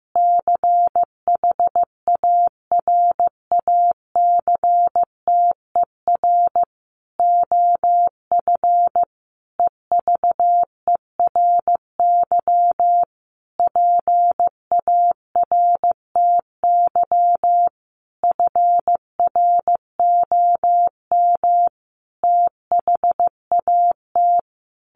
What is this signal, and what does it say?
Morse code, heard as a keyed tone, about 15 wpm, 700 Hz. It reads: CHARACTER OF EVERY PARTY FROM THAT
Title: Sentences from Top 1400 Words 15wpm